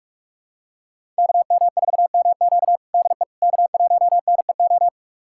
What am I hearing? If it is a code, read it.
Morse code, heard as a keyed tone, about 46 wpm, 700 Hz. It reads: KM4MQ DE K1NEO
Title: USA Callsign DE Callsign - 5 Characters - 45wpm